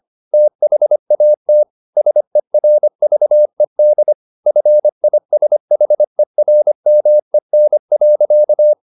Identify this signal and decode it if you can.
Morse code, heard as a keyed tone, about 25 wpm, 600 Hz. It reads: THAT SERVED FISHERMEN.
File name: the old man and the sea part 1 25wpm